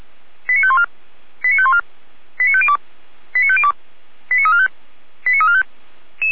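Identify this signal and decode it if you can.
Five-tone selective call (ZVEI), heard as quick runs of five tones, tones of about 70 ms. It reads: 89514 89514 89502 89502 89345 89345
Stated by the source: ZVEI five-tone selective call received by SDR.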